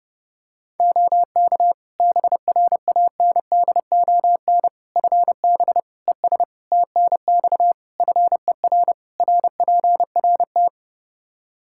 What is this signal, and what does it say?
Morse code, heard as a keyed tone, about 30 wpm, 700 Hz. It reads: OK BRANDON FB ES TNX FER RPRT
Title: K1USN SST QSO Practice 30wpm